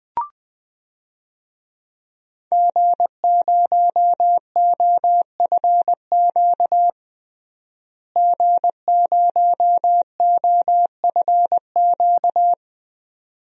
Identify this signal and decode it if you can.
Morse code, heard as a keyed tone, about 20 wpm, 700 Hz. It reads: G0OFQ G0OFQ
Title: DXCC-100 Call Signs - 5 Characters - Repeated 2x - 20wpm